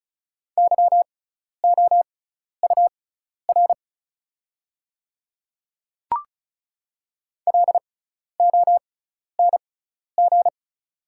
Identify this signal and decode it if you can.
Morse code, heard as a keyed tone, about 36 wpm, spaced as for 12 wpm, 700 Hz. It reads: YOUR LONG
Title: 6x Character Spacing - Top 100 Words - 35wpm